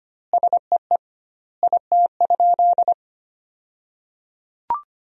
Morse code, heard as a keyed tone, about 25 wpm, 700 Hz. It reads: SEE IT?
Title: 2x Word-Spacing - Sentences from Top 100 Words - 25wpm